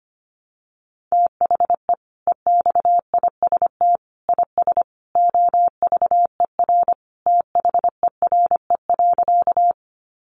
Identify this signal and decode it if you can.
Morse code, heard as a keyed tone, about 25 wpm, 700 Hz. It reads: THE EXIST IS OVER THERE.
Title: Sentences from Top 1100 Words 25wpm